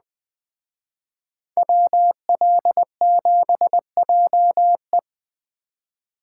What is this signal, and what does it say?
Morse code, heard as a keyed tone, about 20 wpm, 700 Hz. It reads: WL7JE
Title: Speed Racing - USA 2x2 Call Signs 17wpm